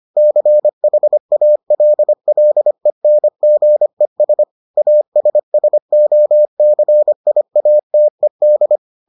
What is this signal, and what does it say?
Morse code, heard as a keyed tone, about 25 wpm, 600 Hz. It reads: CHALLENGES ASSOCIATED